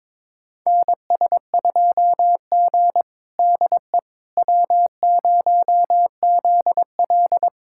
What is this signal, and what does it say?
Morse code, heard as a keyed tone, about 21 wpm, 700 Hz. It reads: NS2G DE W0ZL
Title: USA Callsign DE Callsign - 4 Characters - 22wpm